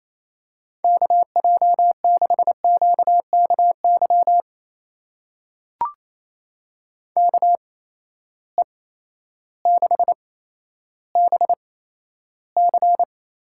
Morse code, heard as a keyed tone, about 28 wpm, spaced as for 13 wpm, 700 Hz. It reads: KJ6QKY KE6BC
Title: USA Call Signs - 6 Characters - 8x Character Spacing - 28wpm